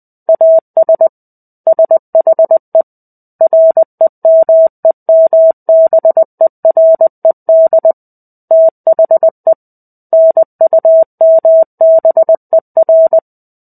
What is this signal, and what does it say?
Morse code, heard as a keyed tone, about 20 wpm, 650 Hz. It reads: AS SHE REMEMBERED THE NUMBER
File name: 05-Chapter-05 20 WPM